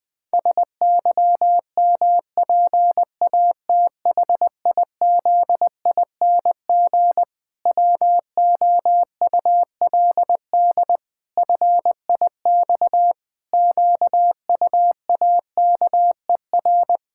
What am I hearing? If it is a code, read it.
Morse code, heard as a keyed tone, about 20 wpm, 700 Hz. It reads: SYMPATHIZING WOULD FIX QUAKER
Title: Pangram Challenge v2 - 20wpm